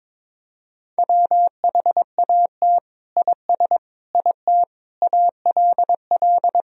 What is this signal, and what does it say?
Morse code, heard as a keyed tone, about 22 wpm, 700 Hz. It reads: WHAT IS IT ALL
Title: Sentences from Top 100 Words and Abbreviations 22wpm